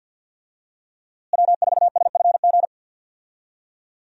Morse code, heard as a keyed tone, about 50 wpm, 700 Hz. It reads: W4SFG
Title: USA Call Signs - 5 Characters - Repeated 2x - 50wpm